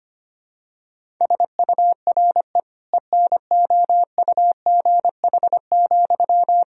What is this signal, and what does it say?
Morse code, heard as a keyed tone, about 25 wpm, 700 Hz. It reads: SURE ENOUGH,
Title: Sentences from Top 1000 Words - Review Part 2 - 25wpm